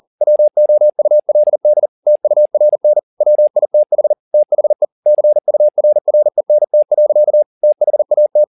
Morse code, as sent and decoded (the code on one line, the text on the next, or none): .-- --- ..- .-.. -.. / - ..- .-. -. / .-- .. - .... / - .... / -.-. ..- .-. .-. -. - .-.-.- / - .... .- -
WOULD TURN WITH THE CURRENT. THAT